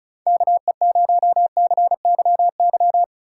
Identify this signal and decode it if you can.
Morse code, heard as a keyed tone, about 36 wpm, 700 Hz. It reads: KE0CYY
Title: Speed Racing - USA 2x3 Call Signs 30wpm